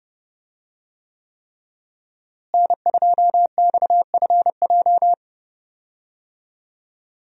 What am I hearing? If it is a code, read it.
Morse code, heard as a keyed tone, about 31 wpm, 700 Hz. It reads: N2XFJ
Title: USA Call Signs - 5 Characters - Repeated 3x - 30wpm